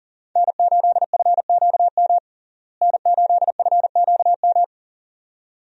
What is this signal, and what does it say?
Morse code, heard as a keyed tone, about 40 wpm, 700 Hz. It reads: N8FQM N8FQM
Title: USA Call Signs - 5 Characters - Repeated 2x - 40wpm